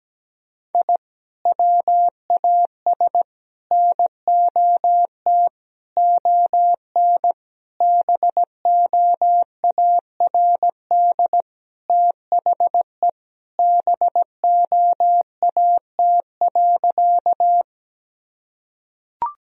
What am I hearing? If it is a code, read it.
Morse code, heard as a keyed tone, about 17 wpm, 700 Hz. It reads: I WAS NOT ON BOARD THE BOAT.
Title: Sentences from Top 1000 Words - Review Part 4 - 17wpm